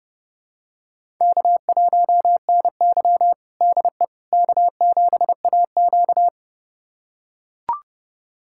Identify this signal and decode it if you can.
Morse code, heard as a keyed tone, about 30 wpm, 700 Hz. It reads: K1NY DE K7AQ
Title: USA Callsign DE Callsign - 4 Characters - 30wpm